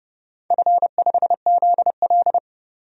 Morse code, heard as a keyed tone, about 32 wpm, 700 Hz. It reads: F5ZL
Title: DXCC-100 Call Signs - 4 Characters - 30wpm